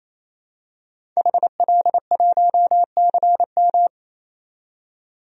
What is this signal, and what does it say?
Morse code, heard as a keyed tone, about 28 wpm, 700 Hz. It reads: HL1CM
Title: DXCC-100 Call Signs - 5 Characters - 28wpm